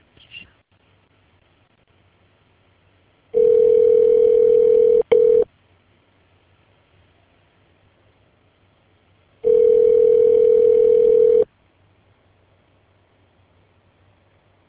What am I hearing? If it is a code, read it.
Ringback tone (North America), 440 + 480 Hz.